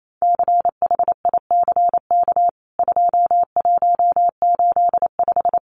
Morse code, heard as a keyed tone, about 28 wpm, 700 Hz. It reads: CHICK 2185
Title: CWT Exchange 28wpm